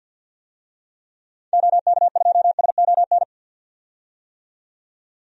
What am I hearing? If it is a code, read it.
Morse code, heard as a keyed tone, about 52 wpm, 700 Hz. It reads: OK2SON
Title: DXCC-100 Call Signs - 6 Characters - 50wpm